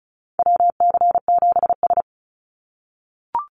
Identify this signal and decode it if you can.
Morse code, heard as a keyed tone, about 34 wpm, 700 Hz. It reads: WC7S
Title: Call Signs - 35wpm